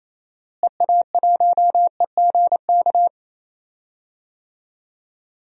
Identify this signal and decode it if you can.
Morse code, heard as a keyed tone, about 28 wpm, 700 Hz. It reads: EA1EGK
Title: DXCC-100 Call Signs - 6 Characters - Repeated 2x - 28wpm